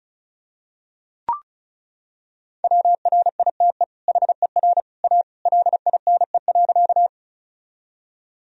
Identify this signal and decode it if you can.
Morse code, heard as a keyed tone, about 34 wpm, 700 Hz. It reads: WRITE HER A LINE.
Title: Sentences from Top 1000 Words - Review Part 1 - 35wpm